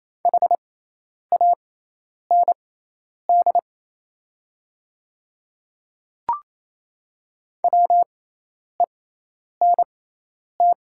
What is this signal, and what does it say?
Morse code, heard as a keyed tone, about 28 wpm, spaced as for 10 wpm, 700 Hz. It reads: HAND WENT
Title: 6x Character Spacing - Top 200 Words - 28wpm